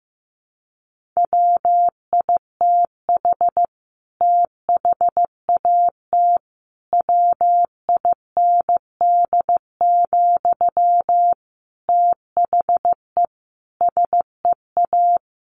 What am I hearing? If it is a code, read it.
Morse code, heard as a keyed tone, about 15 wpm, 700 Hz. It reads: WITH THAT WIND, THE SEA
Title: Sentences from Top 1400 Words 15wpm